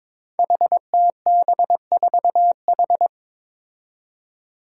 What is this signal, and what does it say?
Morse code, heard as a keyed tone, about 22 wpm, 700 Hz. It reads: HTB4H
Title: Five Challenging Letter-Number Combinations 22wpm